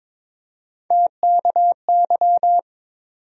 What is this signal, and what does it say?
Morse code, heard as a keyed tone, about 22 wpm, 700 Hz. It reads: TKY
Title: USA Call Sign Postfixes - 3 Letters 22wpm